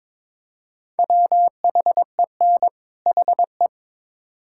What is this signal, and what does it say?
Morse code, heard as a keyed tone, about 22 wpm, 700 Hz. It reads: WHEN HE
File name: Build A Phrase - 2-Word Phrases - 22wpm